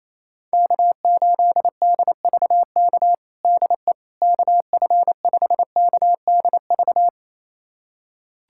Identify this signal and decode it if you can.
Morse code, heard as a keyed tone, about 27 wpm, 700 Hz. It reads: K8DVK DE KF5KDV
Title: USA Callsign DE Callsign 28wpm